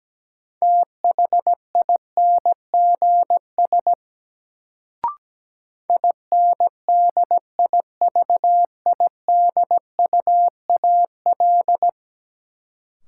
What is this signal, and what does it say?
Morse code, heard as a keyed tone, about 17 wpm, 700 Hz. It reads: THINGS INDIVIDUAL
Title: Sentences from Top 1000 Words - Review Part 4 - New Word Drill - 17wpm